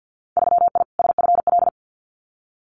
Morse code, heard as a keyed tone, about 50 wpm, 700 Hz. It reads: 3I SFL
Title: Winter Field Day Exchanges 50wpm